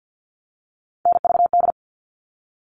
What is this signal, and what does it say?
Morse code, heard as a keyed tone, about 50 wpm, 700 Hz. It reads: NVD
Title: All Cut Numbers - 3-Digit 50wpm